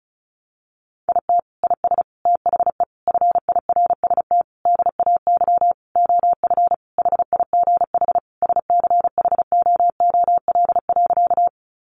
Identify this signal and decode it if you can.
Morse code, heard as a keyed tone, about 34 wpm, 700 Hz. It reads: IT IS THE FIRST DAY OF HIGH SCHOOL.
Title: Additional Easy Phrases 35wpm